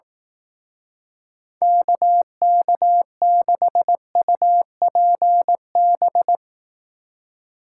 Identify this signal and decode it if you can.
Morse code, heard as a keyed tone, about 18 wpm, 700 Hz. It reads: KK6UPB
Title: Speed Racing - USA 2x3 Call Signs 15wpm